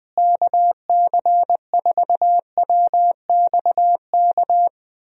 Morse code, heard as a keyed tone, about 21 wpm, 700 Hz. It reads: KC4WXK
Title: USA Call Signs - 6 Characters - Repeated 3x - 20wpm